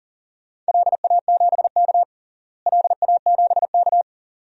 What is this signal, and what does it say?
Morse code, heard as a keyed tone, about 39 wpm, 690 Hz. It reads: LA7K LA7K
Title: DXCC-100 Call Signs - 4 Characters - Repeated 2x - 40wpm